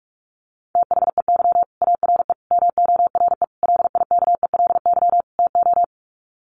Morse code, heard as a keyed tone, about 44 wpm, 700 Hz. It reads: THEY ARE MORE LIKELY TO
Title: Sets of 5 Words Encore2 45wpm